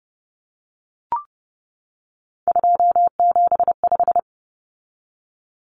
Morse code, heard as a keyed tone, about 30 wpm, 700 Hz. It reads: 275
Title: Common Cut Numbers - 3-Digit 30wpm